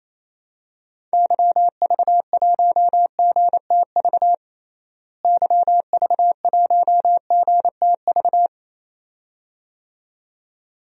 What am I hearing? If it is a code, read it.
Morse code, heard as a keyed tone, about 27 wpm, 700 Hz. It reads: YV1GTV YV1GTV
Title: DXCC-100 Call Signs - 6 Characters - Repeated 2x - 28wpm